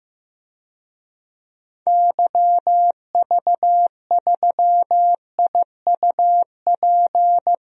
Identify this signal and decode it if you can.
Morse code, heard as a keyed tone, about 15 wpm, 700 Hz. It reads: YV3IUP
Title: DXCC-100 Call Signs - 6 Characters - Repeated 3x - 15wpm